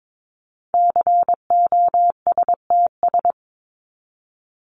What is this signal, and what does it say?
Morse code, heard as a keyed tone, about 22 wpm, 700 Hz. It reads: COSTS
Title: ES:S Suffix - Top 100 Words with Base Words 22wpm